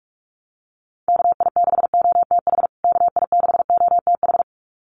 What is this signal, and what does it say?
Morse code, heard as a keyed tone, about 44 wpm, 700 Hz. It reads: KI6OTH KI6OTH
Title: USA Call Signs Repeated 2x 45wpm